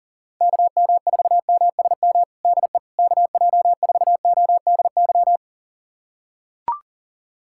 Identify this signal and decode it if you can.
Morse code, heard as a keyed tone, about 40 wpm, 700 Hz. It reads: KM4MSM DE KJ4ODY
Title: USA Callsign DE Callsign 40wpm